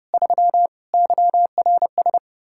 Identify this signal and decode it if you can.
Morse code, heard as a keyed tone, about 34 wpm, 700 Hz. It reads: 3 YRS
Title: Speed-Racing Common QSO Phrases - 30wpm